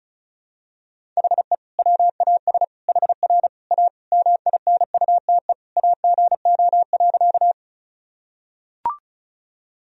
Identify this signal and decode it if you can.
Morse code, heard as a keyed tone, about 36 wpm, 700 Hz. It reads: HE WAS HR A MINUTE AGO.
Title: Sentences from Top 500 Words and Abbreviations 35wpm